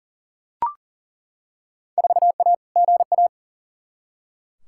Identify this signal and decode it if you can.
Morse code, heard as a keyed tone, about 40 wpm, 700 Hz. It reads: 4A GA
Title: Field Day Exchanges - 40wpm